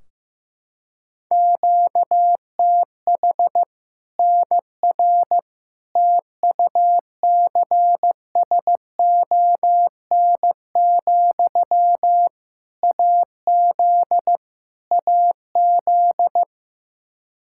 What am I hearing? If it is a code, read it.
Morse code, heard as a keyed tone, about 16 wpm, 700 Hz. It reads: QTH NR TUCSON, AZ AZ
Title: Practice QSO - Exchange 1 - Beginning - 15wpm